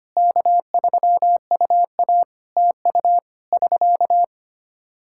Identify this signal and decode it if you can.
Morse code, heard as a keyed tone, about 25 wpm, 700 Hz. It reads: K3UA TU <SK>
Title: K1USN SST QSO Practice 25wpm